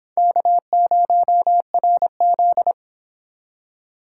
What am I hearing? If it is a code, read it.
Morse code, heard as a keyed tone, about 26 wpm, 700 Hz. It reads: K0RZ